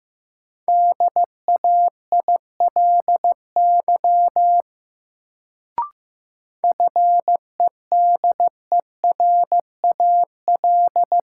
Morse code, heard as a keyed tone, about 15 wpm, 700 Hz. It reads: DAILY FEDERAL
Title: Sentences from Top 1100 Words - New Word Drill Alt - 15wpm